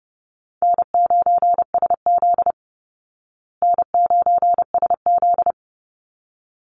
Morse code, heard as a keyed tone, about 30 wpm, 700 Hz. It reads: N9SZ N9SZ
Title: Easy Call Signs Repeated 2x - 30wpm